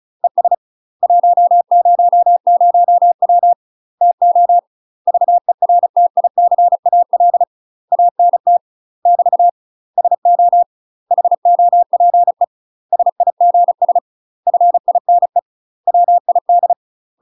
Morse code, heard as a keyed tone, about 35 wpm, 700 Hz. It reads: ES 100W TO VERTICAL ANT <BT> SO HOPE SIGS FINE WID
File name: UK MASTER QSOs TRAINING / 35WPM / UK MASTER QSOs 0023